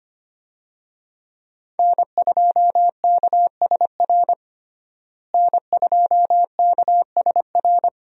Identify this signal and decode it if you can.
Morse code, heard as a keyed tone, about 25 wpm, 700 Hz. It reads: N2KSR N2KSR
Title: USA Call Signs - 5 Characters - Repeated 3x - 25wpm